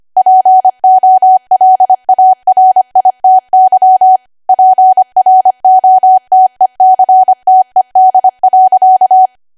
Morse code, heard as a keyed tone, about 25 wpm, 750 Hz. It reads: POLARITY PROTECTED.
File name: cwa / W1AW-Level3 / 25WPM